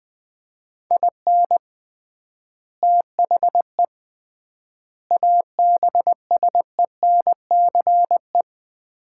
Morse code, heard as a keyed tone, about 20 wpm, 700 Hz. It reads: IN THE ABSENCE
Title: 3x Word-Spacing - Sets of 4 Words - 20wpm